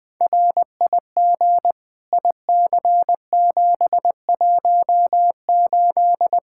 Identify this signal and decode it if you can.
Morse code, heard as a keyed tone, about 21 wpm, 700 Hz. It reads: RIG IC718
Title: K1USN SST QSO Practice 20wpm